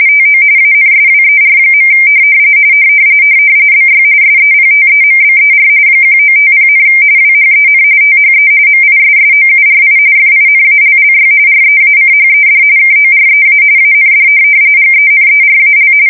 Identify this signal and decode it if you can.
RTTY, about 45.45 baud. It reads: TO ANSWER. BY SEPTEMBER 1950, WHEN THE MISSOURI FINALLY REACHED KOREA AND GENERAL
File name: fsk170 / USS-New-Jersey